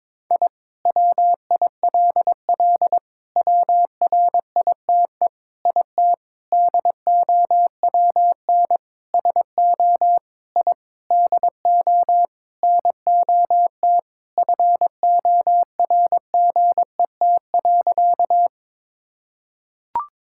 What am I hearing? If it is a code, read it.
Morse code, heard as a keyed tone, about 22 wpm, 700 Hz. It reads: I WILL WRITE IT DOWN SO I DO NOT FORGET.
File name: practice / 22wpm / Sentences with 4-Word Phrases - 22wpm